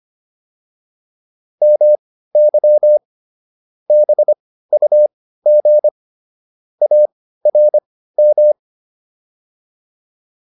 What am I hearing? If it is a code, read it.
Morse code, heard as a keyed tone, about 25 wpm, spaced as for 15 wpm, 600 Hz. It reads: MY BUG ARM